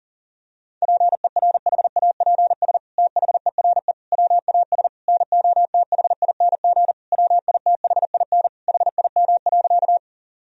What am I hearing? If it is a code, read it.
Morse code, heard as a keyed tone, about 40 wpm, 700 Hz. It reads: PERHAPS THERE WAS NOTHING WITHIN HIM.
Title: Sentences from Top 1000 Words - Review Part 4 - 40wpm